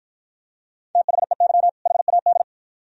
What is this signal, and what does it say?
Morse code, heard as a keyed tone, about 53 wpm, 700 Hz. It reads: THEY HAD